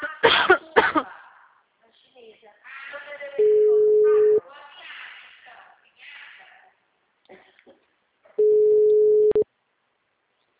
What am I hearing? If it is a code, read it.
Ringback tone (Europe, 425 Hz), with speech.
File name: out- -2030- - - .43096